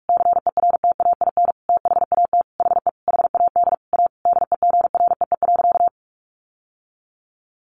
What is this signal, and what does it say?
Morse code, heard as a keyed tone, about 44 wpm, 700 Hz. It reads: CERTAIN THAT HE HAD A DEGREE.
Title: Sentences from Top 1500 Words 45wpm